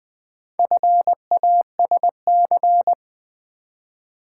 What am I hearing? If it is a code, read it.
Morse code, heard as a keyed tone, about 20 wpm, 700 Hz. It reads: FASC
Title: Build A Word - 4 Letters At A Time - 11-Letter Words - 20wpm